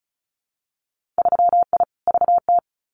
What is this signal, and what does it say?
Morse code, heard as a keyed tone, about 36 wpm, 700 Hz. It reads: 3I VT
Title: Winter Field Day Exchanges Encore 35wpm